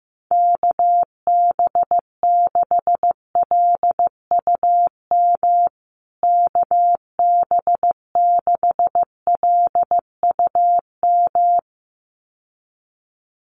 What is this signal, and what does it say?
Morse code, heard as a keyed tone, about 15 wpm, 700 Hz. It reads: KB6LUM KB6LUM
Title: USA Call Signs Repeated 2x 15wpm